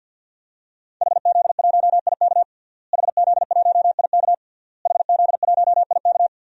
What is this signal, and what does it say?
Morse code, heard as a keyed tone, about 53 wpm, 700 Hz. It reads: HZ1IK HZ1IK HZ1IK
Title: DXCC-100 Call Signs - 5 Characters - Repeated 3x - 50wpm